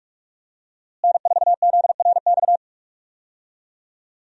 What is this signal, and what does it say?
Morse code, heard as a keyed tone, about 46 wpm, 700 Hz. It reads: N4ZRX
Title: USA Call Signs - 5 Characters - Repeated 3x - 45wpm